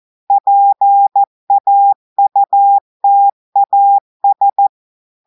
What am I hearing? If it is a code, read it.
Morse code, heard as a keyed tone, about 14 wpm, 800 Hz. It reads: PAUTAS
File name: audios / portugais / portugais texte en clair 2 vitesse 14 mots minutes